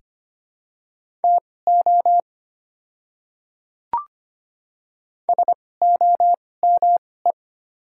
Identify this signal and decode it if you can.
Morse code, heard as a keyed tone, about 25 wpm, spaced as for 18 wpm, 690 Hz. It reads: TO SOME
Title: 2x Character Spacing - Top 100 Words - 25wpm